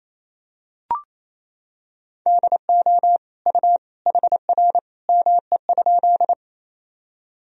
Morse code, heard as a keyed tone, about 28 wpm, 700 Hz. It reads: DO U HR ME?
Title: Sentences from Top 300 Words and Abbreviations 28wpm